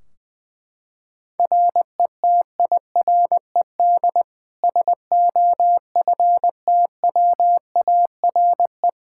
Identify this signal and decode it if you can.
Morse code, heard as a keyed tone, about 20 wpm, 700 Hz. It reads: RETIRED SOFTWARE
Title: Practice QSO - Exchange 3 - More Info - 20wpm